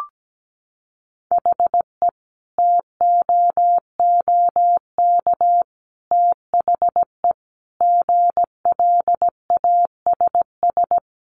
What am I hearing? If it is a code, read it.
Morse code, heard as a keyed tone, about 17 wpm, 700 Hz. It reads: HE TOOK THE GLASS
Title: Sentences from Top 1300 Words 17wpm